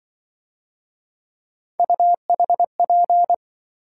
Morse code, heard as a keyed tone, about 24 wpm, 700 Hz. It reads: UHP
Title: Speed-Racing - Three Letter-Number Combinations - 20wpm